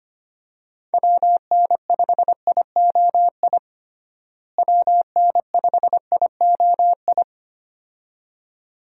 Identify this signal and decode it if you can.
Morse code, heard as a keyed tone, about 25 wpm, 700 Hz. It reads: WN5IOI WN5IOI
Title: USA Call Signs - 6 Characters - Repeated 2x - 25wpm